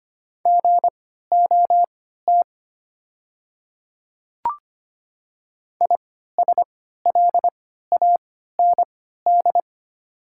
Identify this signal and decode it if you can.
Morse code, heard as a keyed tone, about 25 wpm, spaced as for 14 wpm, 690 Hz. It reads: GOT ISLAND
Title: 3x Character Spacing - Top 500 Words - 25wpm